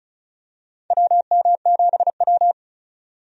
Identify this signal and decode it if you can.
Morse code, heard as a keyed tone, about 36 wpm, 700 Hz. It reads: WM7W
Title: USA Call Signs - 4 Characters - 35wpm